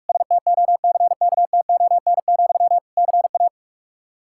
Morse code, heard as a keyed tone, about 42 wpm, 700 Hz. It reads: STOCKTON, CA
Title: Practice QSO - Exchange 1 - Beginning - 45wpm